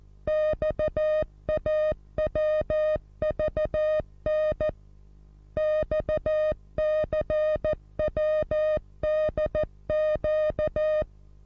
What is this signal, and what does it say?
Morse code, heard as a keyed tone, about 14 wpm, 620 Hz. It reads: XAWVN XCWDQ
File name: repos / Morsekurs / DJ4UF Morsen27